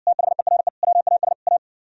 Morse code, heard as a keyed tone, about 69 wpm, 700 Hz. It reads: THERE WAS A